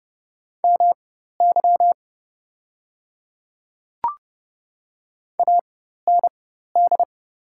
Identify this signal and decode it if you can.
Morse code, heard as a keyed tone, about 30 wpm, spaced as for 14 wpm, 700 Hz. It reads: MY AND
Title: 4x Character Spacing - Top 100 Words - 30wpm